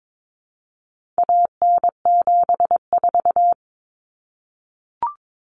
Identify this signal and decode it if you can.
Morse code, heard as a keyed tone, about 22 wpm, 700 Hz. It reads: AN74
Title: Common Cut Numbers - 4-Digit 22wpm